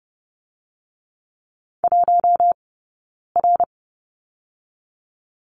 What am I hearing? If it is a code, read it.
Morse code, heard as a keyed tone, about 30 wpm, 700 Hz. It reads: 1 R
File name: course / 30:10wpm / 086 Lesson 15d - Number 5 - Two Characters 30wpm